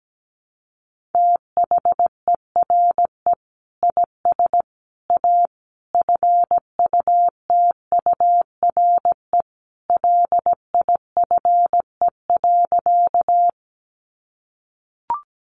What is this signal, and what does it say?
Morse code, heard as a keyed tone, about 17 wpm, 700 Hz. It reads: THERE IS A FUTURE LIFE.
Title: Sentences from Top 700 Words 17wpm